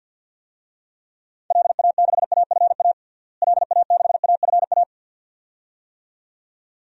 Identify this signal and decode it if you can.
Morse code, heard as a keyed tone, about 50 wpm, 700 Hz. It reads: LA6AFA LA6AFA
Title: DXCC-100 Call Signs - 6 Characters - Repeated 2x - 50wpm